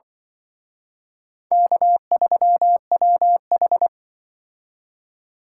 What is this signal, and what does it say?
Morse code, heard as a keyed tone, about 25 wpm, 700 Hz. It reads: K3WH